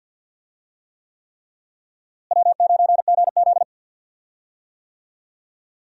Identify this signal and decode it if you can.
Morse code, heard as a keyed tone, about 46 wpm, 700 Hz. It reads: W9GZ
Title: USA Call Signs - 4 Characters - Repeated 3x - 50wpm